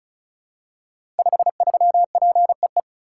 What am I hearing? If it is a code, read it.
Morse code, heard as a keyed tone, about 36 wpm, 700 Hz. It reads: 53PEE
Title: Code Groups - 5 Letter:Numbers - 35wpm